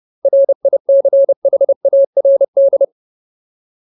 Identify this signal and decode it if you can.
Morse code, heard as a keyed tone, about 30 wpm, 550 Hz. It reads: RICHARD
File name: Ultimate Word List / 30WPM Ultimate